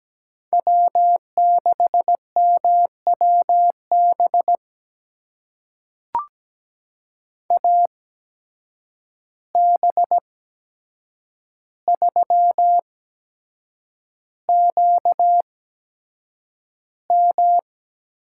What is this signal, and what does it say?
Morse code, heard as a keyed tone, about 17 wpm, spaced as for 7 wpm, 700 Hz. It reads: W6MWB AB3QM
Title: USA Call Signs - 5 Characters - 8x Character Spacing - 17wpm